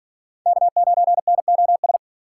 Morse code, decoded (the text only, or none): K9NOS